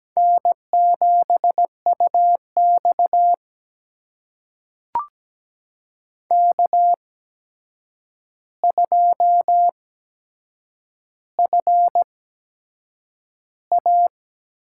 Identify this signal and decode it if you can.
Morse code, heard as a keyed tone, about 17 wpm, spaced as for 7 wpm, 700 Hz. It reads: N7UX K2FA
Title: USA Call Signs - 4 Characters - 8x Character Spacing - 17wpm